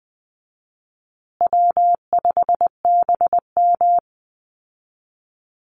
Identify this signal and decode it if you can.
Morse code, heard as a keyed tone, about 20 wpm, 700 Hz. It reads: W5BM
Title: Easy Call Signs Repeated 3x - 20wpm